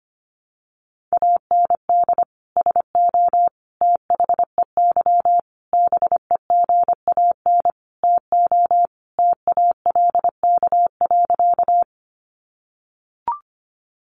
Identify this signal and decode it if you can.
Morse code, heard as a keyed tone, about 25 wpm, 700 Hz. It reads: AND SO THEY BEGAN TO TALK.
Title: Sentences from Top 400 Words 25wpm